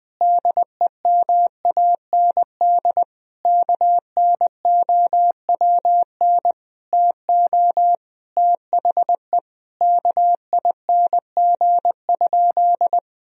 Morse code, heard as a keyed tone, about 21 wpm, 700 Hz. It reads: DEMAND KNOWN TO THE KING?
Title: Sentences from Top 1000 Words - Review Part 4 - 20wpm